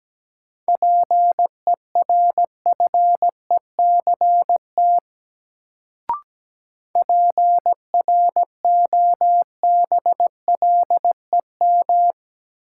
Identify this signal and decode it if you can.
Morse code, heard as a keyed tone, about 17 wpm, 700 Hz. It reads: PERFECT PROBLEM
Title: Sentences from Top 1000 Words - New Word Drill - 17wpm